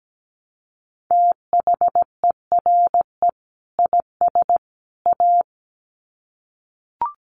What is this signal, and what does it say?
Morse code, heard as a keyed tone, about 17 wpm, 700 Hz. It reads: THERE IS A
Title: Sets of 3 Words 17wpm